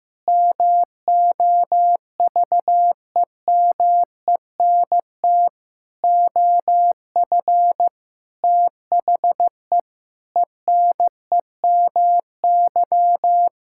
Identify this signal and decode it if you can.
Morse code, heard as a keyed tone, about 16 wpm, 700 Hz. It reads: MOVEMENT OF THE ENEMY
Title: Pangram Challenge v2 - 15wpm